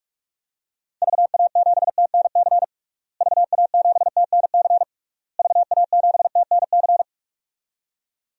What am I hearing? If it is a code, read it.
Morse code, heard as a keyed tone, about 46 wpm, 700 Hz. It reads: VA7TNC VA7TNC VA7TNC
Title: DXCC-100 Call Signs - 6 Characters - Repeated 3x - 45wpm